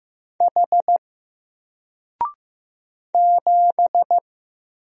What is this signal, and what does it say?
Morse code, heard as a keyed tone, about 15 wpm, 700 Hz. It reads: H 7
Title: Single Letter-Number 15wpm